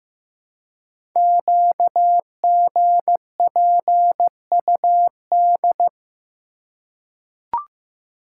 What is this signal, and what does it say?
Morse code, heard as a keyed tone, about 15 wpm, 700 Hz. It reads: QGPUD
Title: Code Groups - 5 Letters - 15wpm